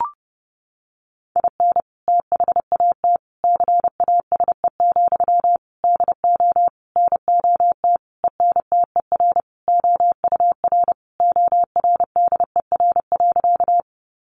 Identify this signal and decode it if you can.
Morse code, heard as a keyed tone, about 30 wpm, 700 Hz. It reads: IN THAT CASE, DO NOT ENTER OUR ORDER.
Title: Sentences from Top 1300 Words 30wpm